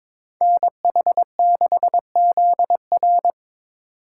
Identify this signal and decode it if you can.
Morse code, heard as a keyed tone, about 22 wpm, 700 Hz. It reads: NH6ZR
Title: USA Call Signs - 5 Characters - 4x Character Spacing - 22wpm